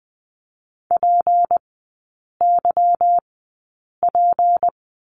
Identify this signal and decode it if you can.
Morse code, heard as a keyed tone, about 20 wpm, spaced as for 8 wpm, 700 Hz. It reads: PYP
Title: 152 Lesson 25e - Letter p - Three Characters 20wpm